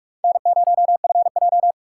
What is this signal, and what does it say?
Morse code, heard as a keyed tone, about 46 wpm, 700 Hz. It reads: N0FJ